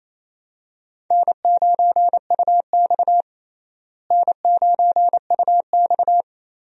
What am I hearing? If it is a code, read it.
Morse code, heard as a keyed tone, about 28 wpm, 700 Hz. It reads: N9UX N9UX
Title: USA Call Signs - 4 Characters - Repeated 3x - 28wpm